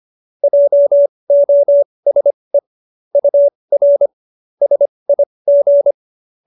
Morse code, heard as a keyed tone, about 25 wpm, spaced as for 20 wpm, 570 Hz. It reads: JOSE UR SIG